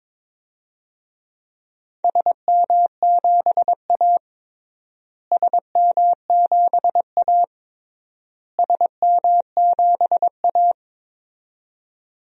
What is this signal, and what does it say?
Morse code, heard as a keyed tone, about 22 wpm, 700 Hz. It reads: SM7A SM7A SM7A
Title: DXCC-100 Call Signs - 4 Characters - Repeated 3x - 22wpm